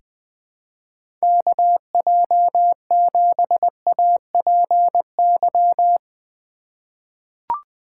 Morse code, heard as a keyed tone, about 21 wpm, 700 Hz. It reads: KJ7APY
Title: USA Call Signs - 4x Character Spacing - 20wpm